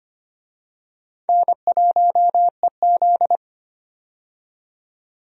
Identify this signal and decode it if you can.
Morse code, heard as a keyed tone, about 25 wpm, 700 Hz. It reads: N1EZ
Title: USA Call Signs - 4 Characters - 25wpm